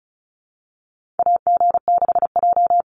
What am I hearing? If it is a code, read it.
Morse code, heard as a keyed tone, about 34 wpm, 700 Hz. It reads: AG6J
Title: USA Call Signs 35wpm